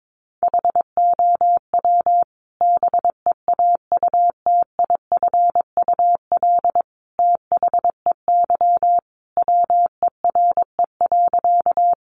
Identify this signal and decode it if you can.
Morse code, heard as a keyed tone, about 22 wpm, 700 Hz. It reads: HOW BEAUTIFUL THEY WERE.
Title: Sentences from Top 900 Words 22wpm